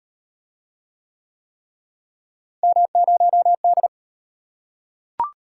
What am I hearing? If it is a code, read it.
Morse code, heard as a keyed tone, about 33 wpm, 700 Hz. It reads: M0D